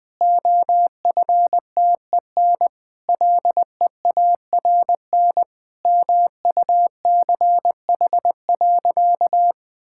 Morse code, heard as a keyed tone, about 20 wpm, 700 Hz. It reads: OFTEN LEARN MUCH.
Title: Sentences with 2-Word Phrases - Spoken Beginning - 20wpm